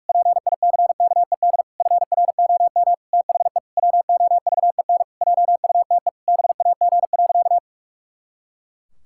Morse code, heard as a keyed tone, about 46 wpm, 700 Hz. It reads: PICKED FROM THE WOVEN JUTE BAG.
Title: Pangram Challenge v2 - 45wpm